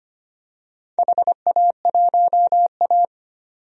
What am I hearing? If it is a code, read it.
Morse code, heard as a keyed tone, about 25 wpm, 700 Hz. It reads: HA1A
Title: DXCC-100 Call Signs - 4 Characters - 25wpm